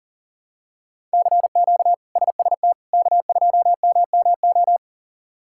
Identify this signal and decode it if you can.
Morse code, heard as a keyed tone, about 37 wpm, 700 Hz. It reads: CQ SST K2MMO
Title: K1USN SST QSO Practice - 40wpm